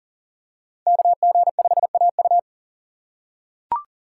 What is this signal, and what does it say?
Morse code, heard as a keyed tone, about 40 wpm, 700 Hz. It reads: KG5AU
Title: USA Call Signs - 5 Characters - 3x Character Spacing - 40wpm